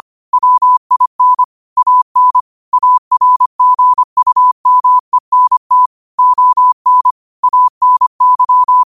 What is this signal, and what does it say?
Morse code, heard as a keyed tone, about 25 wpm, 1000 Hz. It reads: WIN AN ARGUMENT ON ANY
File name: qotdfiles / 25wpm